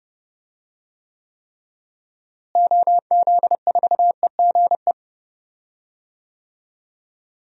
Morse code, heard as a keyed tone, about 30 wpm, 700 Hz. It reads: OZ4EGE
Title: DXCC-100 Call Signs - 6 Characters - Repeated 3x - 30wpm